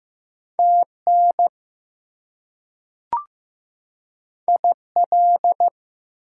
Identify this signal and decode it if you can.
Morse code, heard as a keyed tone, about 15 wpm, 700 Hz. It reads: TN IL
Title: US States and Canadian Provinces 15wpm